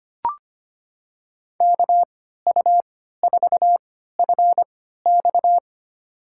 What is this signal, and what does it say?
Morse code, heard as a keyed tone, about 25 wpm, spaced as for 14 wpm, 700 Hz. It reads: KU4FX
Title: USA Call Signs - 5 Characters - 3x Character Spacing - 25wpm